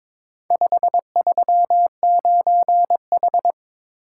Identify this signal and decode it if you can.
Morse code, heard as a keyed tone, about 22 wpm, 700 Hz. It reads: 539H
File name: practice / 22wpm / Four Letter-Number Combinations - 22wpm